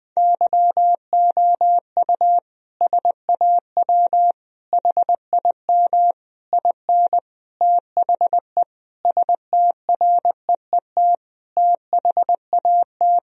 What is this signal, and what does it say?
Morse code, heard as a keyed tone, about 20 wpm, 700 Hz. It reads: YOU SAW HIM IN THE STREET THAT
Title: Sentences from Top 1000 Words - Review Part 3 - 20wpm